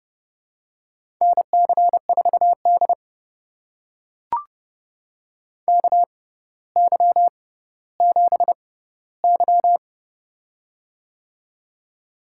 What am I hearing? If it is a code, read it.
Morse code, heard as a keyed tone, about 30 wpm, spaced as for 15 wpm, 700 Hz. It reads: NC4D KY7Y
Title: USA Call Signs - 4 Characters - 6x Character Spacing - 30wpm